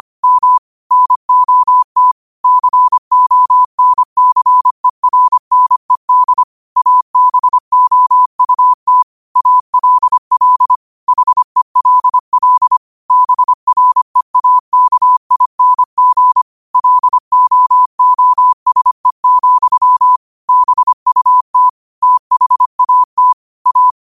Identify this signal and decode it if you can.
Morse code, heard as a keyed tone, about 25 wpm, 1000 Hz. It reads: M NOT CONCERNED ABOUT ALL HELL BREAKING LOOSE, BUT THAT A
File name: qotdfiles / 25wpm